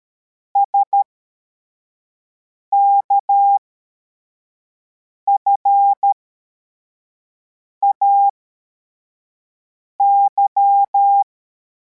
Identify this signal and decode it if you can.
Morse code, heard as a keyed tone, about 13 wpm, spaced as for 4 wpm, 800 Hz. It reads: SKFAY